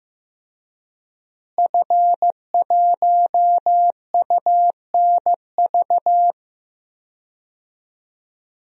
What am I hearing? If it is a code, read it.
Morse code, heard as a keyed tone, about 15 wpm, 700 Hz. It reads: F1UNV
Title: DXCC-100 Call Signs - 5 Characters - Repeated 3x - 15wpm